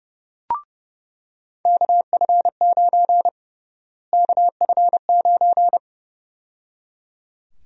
Morse code, heard as a keyed tone, about 30 wpm, 700 Hz. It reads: KF9 KF9
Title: USA Call Sign Prefixes - Repeated 2x - 30wpm